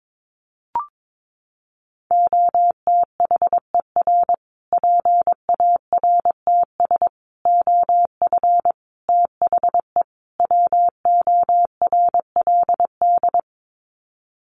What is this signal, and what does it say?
Morse code, heard as a keyed tone, about 22 wpm, 700 Hz. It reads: OTHER PARTS OF THE WORLD
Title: Sets of 5 Words Encore 22wpm